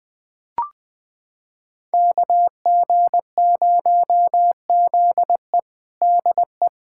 Morse code, heard as a keyed tone, about 20 wpm, 700 Hz. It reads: KG0ZE DE
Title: USA Callsign DE Callsign - 5 Characters - 20wpm